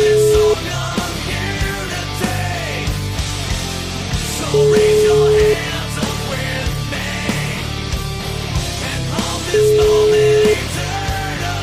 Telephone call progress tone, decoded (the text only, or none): ringback tone